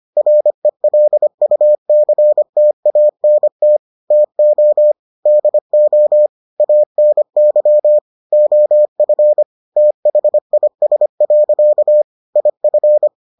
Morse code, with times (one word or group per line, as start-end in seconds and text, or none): 0.2-3.8 RELUCTANT
4.1-4.9 TO
5.3-6.3 DO
6.6-8.0 ANY
8.3-9.4 OF
9.8-12.0 THIS.
12.4-13.1 IF